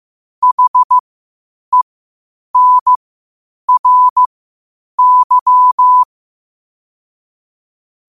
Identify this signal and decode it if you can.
Morse code, heard as a keyed tone, about 15 wpm, spaced as for 8 wpm, 1000 Hz. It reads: HENRY
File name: qotdfiles / 8wpm